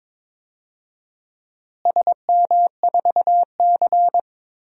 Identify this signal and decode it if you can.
Morse code, heard as a keyed tone, about 22 wpm, 700 Hz. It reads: SM4C
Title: DXCC-100 Call Signs - 4 Characters - Repeated 2x - 22wpm